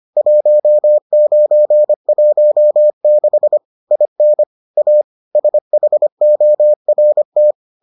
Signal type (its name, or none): Morse code